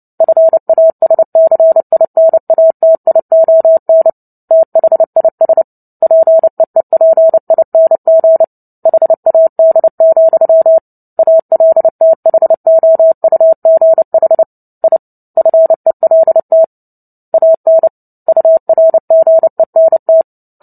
Morse code, heard as a keyed tone, about 30 wpm, 650 Hz. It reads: FASCINATION THIS PEEPING HAD, ALTHOUGH I FELT AN URGENT
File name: Book Two / 03-Chapter 3 - 29 WPM